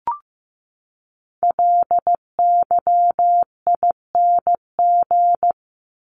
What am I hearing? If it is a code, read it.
Morse code, heard as a keyed tone, about 15 wpm, 700 Hz. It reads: LYING
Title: ING Suffix - Top 100 Words with Base Words 15wpm